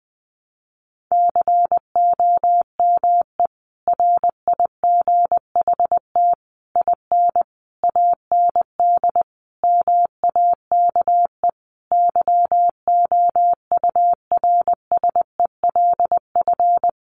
Morse code, heard as a keyed tone, about 20 wpm, 700 Hz. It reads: COME RIGHT IN AND MAKE YOURSELF
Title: Sentences from Top 600 Words 20wpm